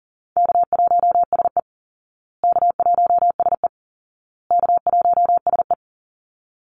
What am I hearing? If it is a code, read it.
Morse code, heard as a keyed tone, about 40 wpm, 700 Hz. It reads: K1SE K1SE K1SE
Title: Call Signs Repeated 2x - 40wpm